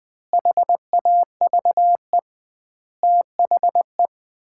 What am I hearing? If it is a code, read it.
Morse code, heard as a keyed tone, about 21 wpm, 700 Hz. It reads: HAVE THE
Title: 2x Word-Spacing - Sets of 5 Words - 20wpm